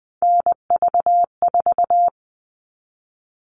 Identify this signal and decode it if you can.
Morse code, heard as a keyed tone, about 20 wpm, 700 Hz. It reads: NV4
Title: All Cut Numbers - 3-Digit 20wpm